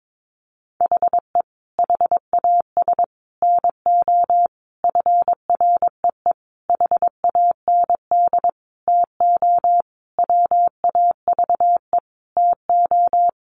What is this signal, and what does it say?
Morse code, heard as a keyed tone, about 22 wpm, 700 Hz. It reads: HE HAS NO FREE HAND TO WAVE TO
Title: Sentences from Top 1400 Words 22wpm